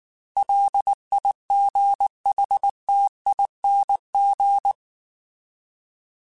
Morse code, heard as a keyed tone, about 19 wpm, 800 Hz. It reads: LIGHTING